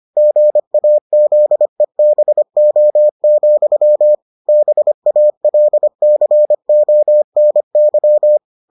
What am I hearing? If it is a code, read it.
Morse code, heard as a keyed tone, about 25 wpm, 600 Hz. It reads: GAZEBO, BALCONY